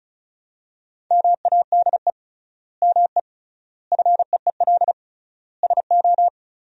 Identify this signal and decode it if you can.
Morse code, heard as a keyed tone, about 36 wpm, 700 Hz. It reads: MADE ME FEEL SO
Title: 3x Word-Spacing - Sentences from Top 400 Words - 35wpm